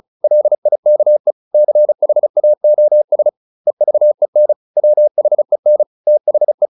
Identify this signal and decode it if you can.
Morse code, heard as a keyed tone, about 34 wpm, 600 Hz. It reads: LIKE CHAOS EVEN WHEN THE